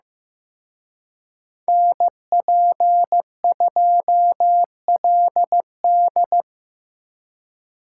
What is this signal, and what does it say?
Morse code, heard as a keyed tone, about 18 wpm, 700 Hz. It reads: NP2LD
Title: Speed Racing - USA Call Signs 15wpm